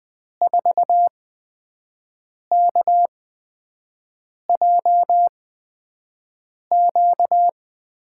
Morse code, heard as a keyed tone, about 20 wpm, spaced as for 5 wpm, 700 Hz. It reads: 4KJQ
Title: USA Call Signs - 5 Characters - 8x Character Spacing - 20wpm